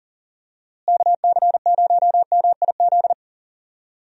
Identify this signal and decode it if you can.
Morse code, heard as a keyed tone, about 40 wpm, 700 Hz. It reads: KC0MIZ
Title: USA Call Signs - 6 Characters - Repeated 2x - 40wpm